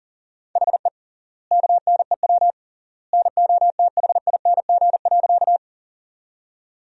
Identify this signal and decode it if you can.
Morse code, heard as a keyed tone, about 40 wpm, 700 Hz. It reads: HE KNEW NOTHING.
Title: 3x Word-Spacing - Sentences from Top 500 Words - 40wpm